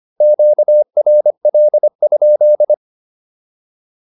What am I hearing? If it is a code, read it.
Morse code, heard as a keyed tone, about 25 wpm, 600 Hz. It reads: QRL?